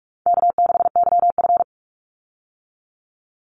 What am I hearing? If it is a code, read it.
Morse code, heard as a keyed tone, about 44 wpm, 700 Hz. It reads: K6YF
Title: Easy Call Signs Repeated 2x - 45wpm